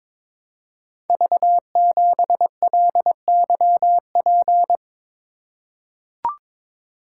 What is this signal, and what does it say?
Morse code, heard as a keyed tone, about 22 wpm, 700 Hz. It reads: V7LYP
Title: Code Groups - 5 Letter:Numbers - 22wpm